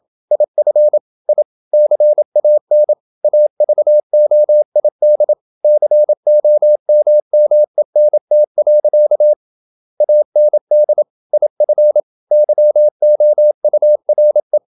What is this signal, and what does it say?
Morse code, heard as a keyed tone, about 27 wpm, 600 Hz. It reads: IF I CAN AVOID COMMENT. AND IF YOURE
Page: Never 26wpm | CW med Gnister